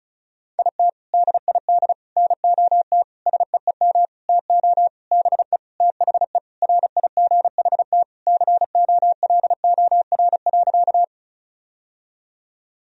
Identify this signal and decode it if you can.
Morse code, heard as a keyed tone, about 36 wpm, 700 Hz. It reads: IT DID NOT SEEM TO BE THE RIGHT COLOR.
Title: Sentences from Top 1000 Words - Review Part 2 - 35wpm